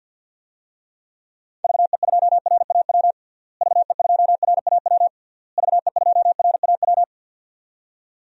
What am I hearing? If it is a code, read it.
Morse code, heard as a keyed tone, about 46 wpm, 700 Hz. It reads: VE2RAW VE2RAW VE2RAW
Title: DXCC-100 Call Signs - 6 Characters - Repeated 3x - 50wpm